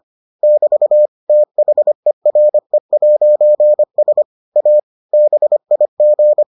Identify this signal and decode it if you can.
Morse code, heard as a keyed tone, about 25 wpm, 600 Hz. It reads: <BT> THERE'S A BIG